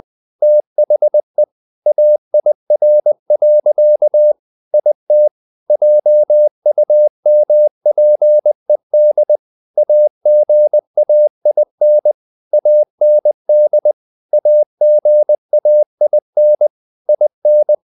Morse code, heard as a keyed tone, about 20 wpm, 600 Hz. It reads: THE AIR. IT JUMPED AGAIN AND AGAIN IN
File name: the old man and the sea part 2 20wpm